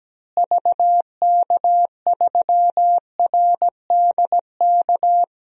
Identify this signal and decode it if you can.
Morse code, heard as a keyed tone, about 17 wpm, 700 Hz. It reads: VK3RDK
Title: DXCC-100 Call Signs - 6 Characters - Repeated 2x - 17wpm